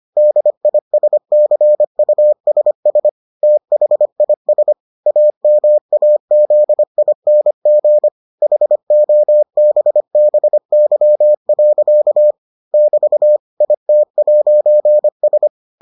Morse code, heard as a keyed tone, about 25 wpm, 600 Hz. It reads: DISCUSS THIS AMAZING HOBBY. <BT> IT'S